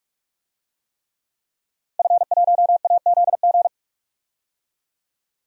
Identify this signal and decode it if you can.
Morse code, heard as a keyed tone, about 46 wpm, 700 Hz. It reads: F1AZG
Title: DXCC-100 Call Signs - 5 Characters - 45wpm